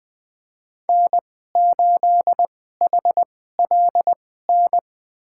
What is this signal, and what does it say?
Morse code, heard as a keyed tone, about 20 wpm, spaced as for 15 wpm, 700 Hz. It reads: N8HLN
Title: USA Call Signs - 5 Characters - 2x Character Spacing - 20wpm